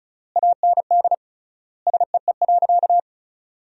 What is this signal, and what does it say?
Morse code, heard as a keyed tone, about 34 wpm, 700 Hz. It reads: AND SEE.
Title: 3x Word-Spacing - Sentences from Top 100 Words - 35wpm